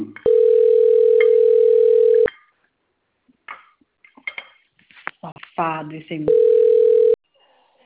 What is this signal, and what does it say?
Ringback tone (North America), 440 + 480 Hz, with speech.
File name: rg-1600-unknown- - - .79639